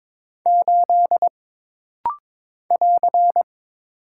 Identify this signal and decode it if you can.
Morse code, heard as a keyed tone, about 22 wpm, 700 Hz. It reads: 8 <AR>
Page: This rapid-fire timing is geared to push your ICR (Instant Character Recognition) skills.